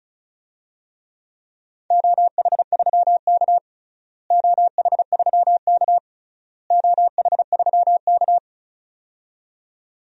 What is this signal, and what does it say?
Morse code, heard as a keyed tone, about 34 wpm, 700 Hz. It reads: OH3K OH3K OH3K
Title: DXCC-100 Call Signs - 4 Characters - Repeated 3x - 35wpm